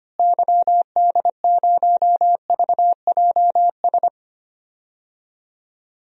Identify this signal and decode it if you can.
Morse code, heard as a keyed tone, about 25 wpm, 700 Hz. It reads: YD0VJS
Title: DXCC-100 Call Signs - 6 Characters - Repeated 2x - 25wpm